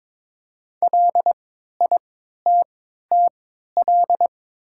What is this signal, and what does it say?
Morse code, heard as a keyed tone, about 22 wpm, spaced as for 13 wpm, 700 Hz. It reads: LITTL
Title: 3x Character Spacing - Top 200 Words - 22wpm